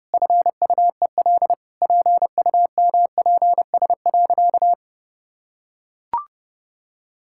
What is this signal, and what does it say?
Morse code, heard as a keyed tone, about 30 wpm, 700 Hz. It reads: FUEL PUMPS.
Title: Pangram Challenge v2 - 30wpm